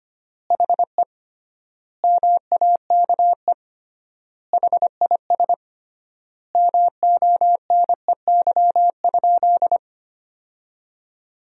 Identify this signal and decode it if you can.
Morse code, heard as a keyed tone, about 25 wpm, 700 Hz. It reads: HE MAKE HIS MONEY?
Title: 3x Word-Spacing - Sentences from Top 400 Words - 25wpm